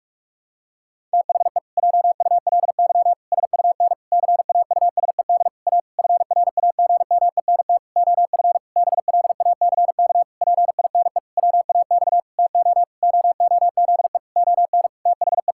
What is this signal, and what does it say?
Morse code, heard as a keyed tone, about 48 wpm, 700 Hz. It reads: THE JULY SUN CAUSED A FRAGMENT OF BLACK PINE WAX TO OOZE ON THE
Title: Pangram Challenge v2 - 45wpm